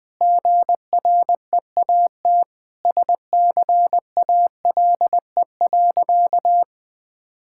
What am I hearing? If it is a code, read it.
Morse code, heard as a keyed tone, about 20 wpm, 700 Hz. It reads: GREAT SCALE.
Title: Sentences from Top 1000 Words - Review Part 5 - 20wpm